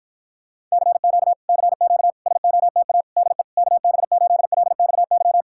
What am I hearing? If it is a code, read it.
Morse code, heard as a keyed tone, about 52 wpm, 700 Hz. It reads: CQ CQ SOTA DE KB8LXY